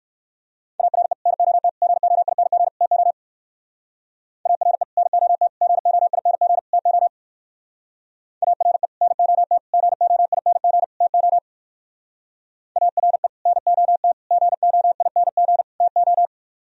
Morse code, heard as a keyed tone, about 58 wpm, 700 Hz. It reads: ARE NOT GOING TO ARE NOT GOING TO ARE NOT GOING TO ARE NOT GOING TO